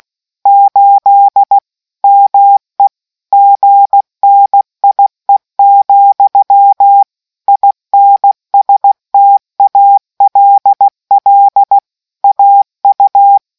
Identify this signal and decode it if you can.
Morse code, heard as a keyed tone, about 16 wpm, 800 Hz. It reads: 8 ME GNIE, INSTALL AU